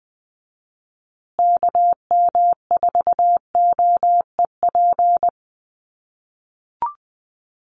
Morse code, heard as a keyed tone, about 20 wpm, 700 Hz. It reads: KM4OEP
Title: USA Call Signs Repeated 3x 20wpm